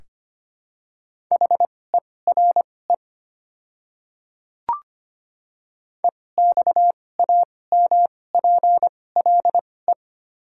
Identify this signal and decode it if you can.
Morse code, heard as a keyed tone, about 25 wpm, spaced as for 18 wpm, 690 Hz. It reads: HERE EXAMPLE
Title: 2x Character Spacing - Top 500 Words - 25wpm